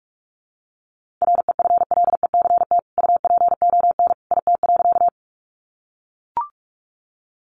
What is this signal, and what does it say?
Morse code, heard as a keyed tone, about 44 wpm, 700 Hz. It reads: REFLECT UPON IT.
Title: Sentences from Top 1500 Words 45wpm